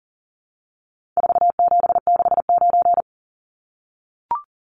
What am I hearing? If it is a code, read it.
Morse code, heard as a keyed tone, about 40 wpm, 700 Hz. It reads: 4769
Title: Numbers - 4-Digit 40wpm